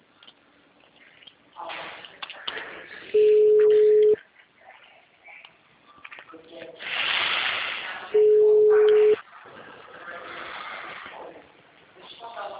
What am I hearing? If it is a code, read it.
Ringback tone (Europe, 425 Hz), with speech.